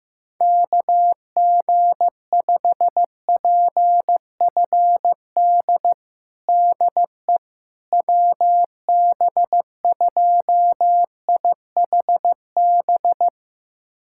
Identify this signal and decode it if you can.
Morse code, heard as a keyed tone, about 16 wpm, 700 Hz. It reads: KG5PFD DE WB2IHB
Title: USA Callsign DE Callsign 15wpm